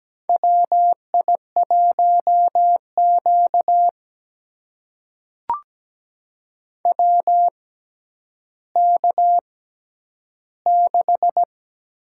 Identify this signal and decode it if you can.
Morse code, heard as a keyed tone, about 17 wpm, spaced as for 10 wpm, 700 Hz. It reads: WI1Q WK6
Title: USA Call Signs - 4 Characters - 6x Character Spacing - 17wpm